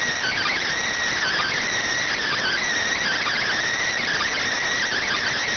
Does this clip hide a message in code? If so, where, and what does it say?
Yes: Emergency Alert System (SAME) tones (header bursts) from 1.2 to 5.4 s, heard as harsh buzzing bursts.